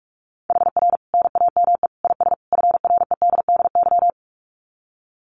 Morse code, heard as a keyed tone, about 44 wpm, 700 Hz. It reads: HR NAME IS FREDDY
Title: Common QSO Phrases 45wpm